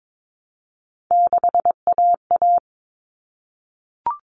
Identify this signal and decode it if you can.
Morse code, heard as a keyed tone, about 22 wpm, 700 Hz. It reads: 6AA
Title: Common Cut Numbers - 3-Digit 22wpm